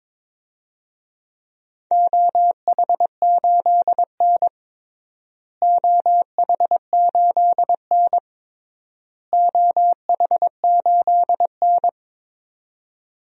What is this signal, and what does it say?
Morse code, heard as a keyed tone, about 22 wpm, 700 Hz. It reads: OH8N OH8N OH8N
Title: DXCC-100 Call Signs - 4 Characters - Repeated 3x - 22wpm